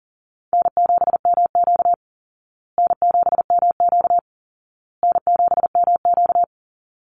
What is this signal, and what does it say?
Morse code, heard as a keyed tone, about 40 wpm, 700 Hz. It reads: N7MQ N7MQ N7MQ
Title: Easy Call Signs Repeated 3x - 40wpm